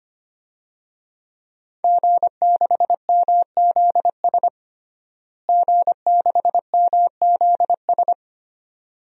Morse code, heard as a keyed tone, about 25 wpm, 700 Hz. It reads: G6MZS G6MZS
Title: DXCC-100 Call Signs - 5 Characters - Repeated 2x - 25wpm